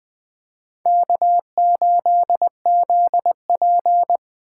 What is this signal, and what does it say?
Morse code, heard as a keyed tone, about 20 wpm, 700 Hz. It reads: K8ZP
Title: Speed Racing - USA 1x2 Call Signs 17wpm